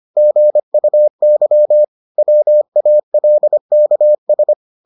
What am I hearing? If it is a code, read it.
Morse code, heard as a keyed tone, about 25 wpm, 600 Hz. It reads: GUY WALKS